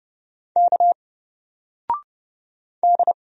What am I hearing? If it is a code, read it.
Morse code, heard as a keyed tone, about 30 wpm, 700 Hz. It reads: K D
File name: Single-Letter-Number-30wpm